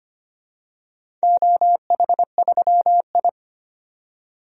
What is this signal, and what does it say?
Morse code, heard as a keyed tone, about 25 wpm, 700 Hz. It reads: OH3I
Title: DXCC-100 Call Signs - 4 Characters - 25wpm